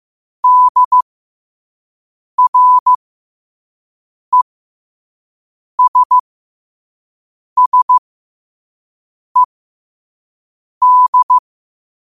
Morse code, heard as a keyed tone, about 15 wpm, spaced as for 5 wpm, 1000 Hz. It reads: DRESSED
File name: qotdfiles / 5wpm